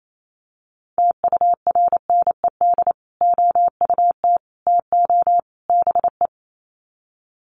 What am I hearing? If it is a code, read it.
Morse code, heard as a keyed tone, about 28 wpm, 700 Hz. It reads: TURNED OUT TO BE
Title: Sets of 4 Words 28wpm